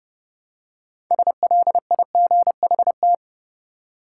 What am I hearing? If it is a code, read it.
Morse code, heard as a keyed tone, about 30 wpm, 700 Hz. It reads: SLIGHT
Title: Six Letter Words - 30wpm